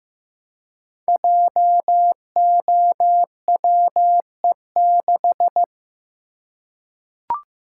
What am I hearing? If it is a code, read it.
Morse code, heard as a keyed tone, about 15 wpm, 700 Hz. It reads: JOWE6
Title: Code Groups - 5 Letter:Numbers - 15wpm